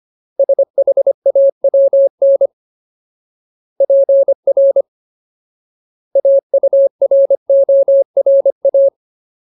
Morse code, heard as a keyed tone, about 25 wpm, 550 Hz. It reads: SHAWN PR AURORA
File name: Ultimate Word List / 25WPM Ultimate